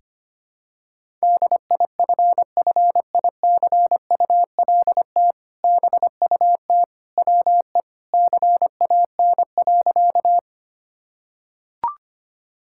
Morse code, heard as a keyed tone, about 25 wpm, 700 Hz. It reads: DIFFICULT BUT WE CAN.
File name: practice / 25wpm / Sentences with 2-Word Phrases - 25wpm